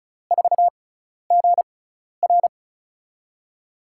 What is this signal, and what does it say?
Morse code, heard as a keyed tone, about 34 wpm, spaced as for 12 wpm, 700 Hz. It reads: 4GR
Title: USA Call Signs - 5 Characters - 6x Character Spacing - 35wpm